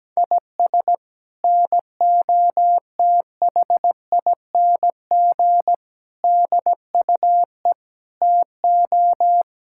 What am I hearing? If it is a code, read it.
Morse code, heard as a keyed tone, about 17 wpm, 700 Hz. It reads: IS NOTHING DUE TO
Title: Sentences from Top 1000 Words - Review Part 6 - 17wpm